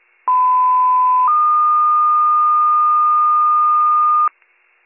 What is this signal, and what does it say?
Two-tone page, heard as a one-second tone, then a three-second tone, about 1010 then 1220 Hz. It is heard from 0.3 to 4.3 s.